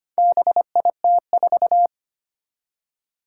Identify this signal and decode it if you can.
Morse code, heard as a keyed tone, about 25 wpm, 700 Hz. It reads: BIT4
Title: Four Challenging Letter-Number Combinations 25wpm